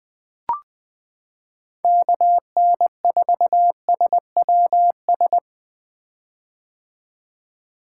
Morse code, heard as a keyed tone, about 20 wpm, 700 Hz. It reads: KN4SWS
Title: 2023 POTA Activator Call Signs - 20wpm